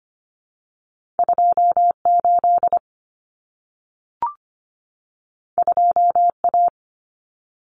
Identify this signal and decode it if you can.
Morse code, heard as a keyed tone, about 25 wpm, 700 Hz. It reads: 28 2A
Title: Common Cut Numbers - 2-Digit 25wpm